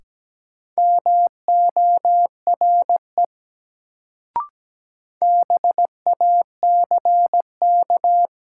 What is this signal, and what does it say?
Morse code, heard as a keyed tone, about 17 wpm, 700 Hz. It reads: MORE BACK
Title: Sets of 2 Words - New Word Drill - 17wpm